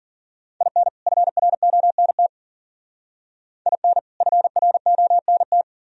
Morse code, heard as a keyed tone, about 43 wpm, 700 Hz. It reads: IN FRONT IN FRONT
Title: Speed-Racing - Sets of 2 Words 40wpm